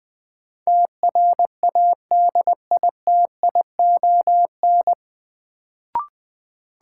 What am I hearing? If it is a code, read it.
Morse code, heard as a keyed tone, about 20 wpm, 700 Hz. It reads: TRADITION
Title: Sentences from Top 1500 Words - New Word Drill Alt - 20wpm